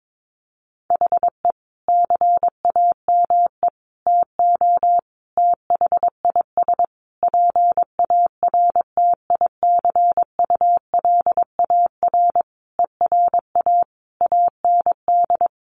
Morse code, heard as a keyed tone, about 22 wpm, 700 Hz. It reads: HE CAME TO THIS PARTICULAR ERA AND
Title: A Sound Of Thunder 22wpm